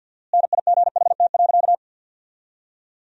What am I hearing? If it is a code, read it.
Morse code, heard as a keyed tone, about 47 wpm, 700 Hz. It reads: NIGHT.
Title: 4x Word-Spacing - Sentences from Top 300 Words - 50wpm